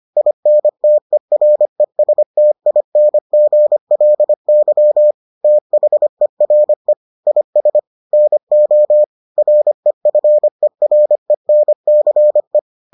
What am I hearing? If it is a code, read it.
Morse code, heard as a keyed tone, about 25 wpm, 600 Hz. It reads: INTERESTINGLY THERE IS NO REFERENCE